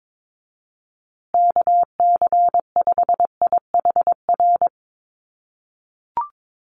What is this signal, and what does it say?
Morse code, heard as a keyed tone, about 22 wpm, 700 Hz. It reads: KC5IHR
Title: USA Call Signs Repeated 3x 22wpm